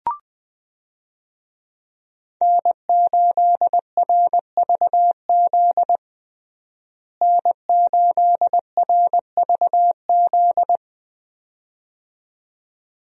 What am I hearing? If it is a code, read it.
Morse code, heard as a keyed tone, about 21 wpm, 700 Hz. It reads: N8RVZ N8RVZ
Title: USA Call Signs - 5 Characters - Repeated 2x - 20wpm